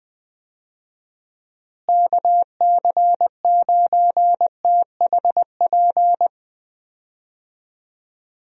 Morse code, heard as a keyed tone, about 20 wpm, 700 Hz. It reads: KC9THP
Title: USA Call Signs - 6 Characters - Repeated 2x - 20wpm